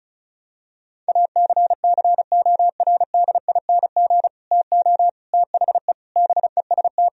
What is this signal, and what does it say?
Morse code, heard as a keyed tone, about 34 wpm, 700 Hz. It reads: ACCORDING TO THE BEST
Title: Sentences from Top 1000 Words - Review Part 4 - 35wpm